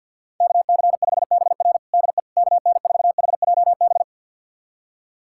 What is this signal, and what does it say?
Morse code, heard as a keyed tone, about 46 wpm, 700 Hz. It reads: KC5BR DE KN4HJB
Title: USA Callsign DE Callsign 50wpm